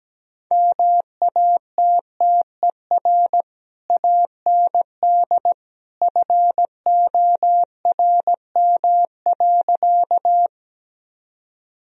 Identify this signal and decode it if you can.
Morse code, heard as a keyed tone, about 17 wpm, 700 Hz. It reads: MATTER AND FORM.
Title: Sentences from Top 1000 Words - Review Part 1 - 17wpm